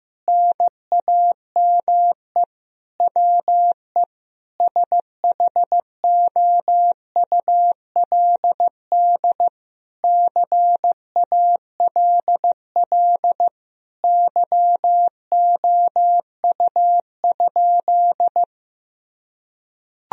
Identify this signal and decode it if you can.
Morse code, heard as a keyed tone, about 15 wpm, 700 Hz. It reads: NAME WE SHOULD CALL YOU?
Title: Sentences from Top 1000 Words - Review Part 3 - 15wpm